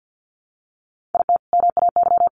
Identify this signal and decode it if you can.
Morse code, heard as a keyed tone, about 50 wpm, 700 Hz. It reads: IT MAY
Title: Sentences from Top 1200 Words 50wpm